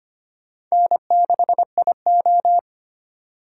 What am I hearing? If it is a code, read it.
Morse code, heard as a keyed tone, about 31 wpm, 700 Hz. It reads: N6IO
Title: Speed Racing - USA Call Signs 25wpm